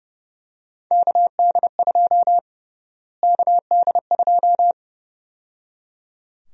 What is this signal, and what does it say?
Morse code, heard as a keyed tone, about 30 wpm, 700 Hz. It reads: KD2 KD2
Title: USA Call Sign Prefixes - Repeated 2x - 30wpm